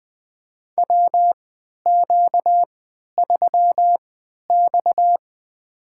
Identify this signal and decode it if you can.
Morse code, heard as a keyed tone, about 20 wpm, spaced as for 11 wpm, 700 Hz. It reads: WQ3X
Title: USA Call Signs - 4 Characters - 3x Character Spacing - 20wpm